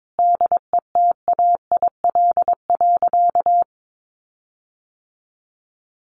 Morse code, heard as a keyed tone, about 22 wpm, 700 Hz. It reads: DETAIL.
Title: Sentences from Top 1500 Words 22wpm